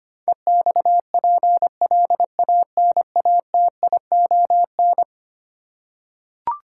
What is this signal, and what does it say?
Morse code, heard as a keyed tone, about 25 wpm, 700 Hz. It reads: EXPLANATION
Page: morse-code-ninja-practice-25wpm